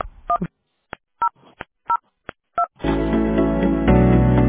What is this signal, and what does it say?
DTMF, heard as telephone touch tones, tones of about 80 ms. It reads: 1002